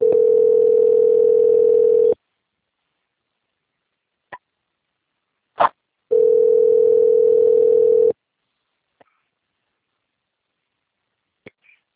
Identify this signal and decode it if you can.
Ringback tone (North America), 440 + 480 Hz.
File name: out- - - - - .21247